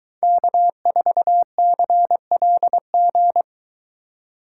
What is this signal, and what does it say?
Morse code, heard as a keyed tone, about 23 wpm, 700 Hz. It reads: K4CLG